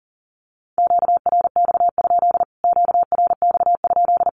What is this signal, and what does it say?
Morse code, heard as a keyed tone, about 40 wpm, 700 Hz. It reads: QRX? QRX?
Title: Q-Codes Repeated 3x 40wpm